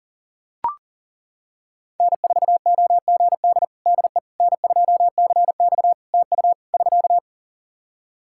Morse code, heard as a keyed tone, about 37 wpm, 700 Hz. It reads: N4OGD DE N2CX TU <SK>
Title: Practice QSO - Exchange 4 - Ending - 40wpm